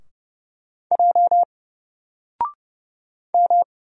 Morse code, heard as a keyed tone, about 30 wpm, 700 Hz. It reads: J M
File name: Single-Letter-Number-30wpm